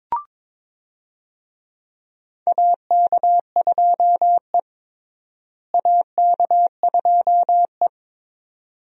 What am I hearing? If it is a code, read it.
Morse code, heard as a keyed tone, about 21 wpm, 700 Hz. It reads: AK2E AK2E
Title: USA Call Signs - 4 Characters - Repeated 2x - 22wpm